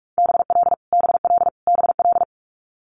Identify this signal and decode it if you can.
Morse code, heard as a keyed tone, about 46 wpm, 700 Hz. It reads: BL BL BL
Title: Multi-letter Phonemes Repeated 3x 45wpm